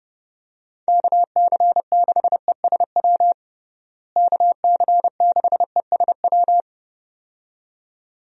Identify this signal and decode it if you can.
Morse code, heard as a keyed tone, about 30 wpm, 700 Hz. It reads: KC6ESW KC6ESW
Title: USA Call Signs - 6 Characters - Repeated 2x - 30wpm